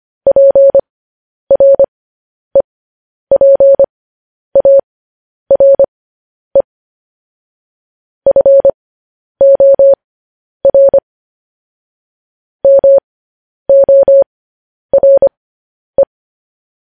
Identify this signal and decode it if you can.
Morse code, heard as a keyed tone, about 25 wpm, spaced as for 10 wpm, 550 Hz. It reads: PREPARE FOR MORE
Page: news headlines in morse code at 10 wpm